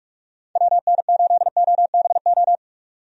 Morse code, heard as a keyed tone, about 46 wpm, 700 Hz. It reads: WN8OBO
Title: USA Call Signs - 6 Characters - 45wpm